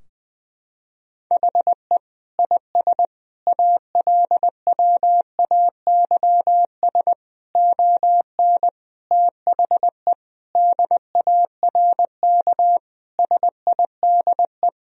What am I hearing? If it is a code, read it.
Morse code, heard as a keyed tone, about 20 wpm, 700 Hz. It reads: HE IS ALWAYS ON THE DARK SIDE
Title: Sentences from Top 1000 Words - Review Part 3 - 20wpm